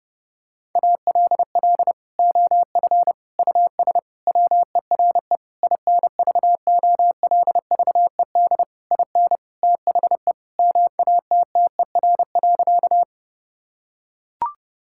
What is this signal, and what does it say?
Morse code, heard as a keyed tone, about 30 wpm, 700 Hz. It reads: ALL OF US WERE INVOLVED IN THE MATTER.
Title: Sentences from Top 1000 Words - Review Part 5 - 30wpm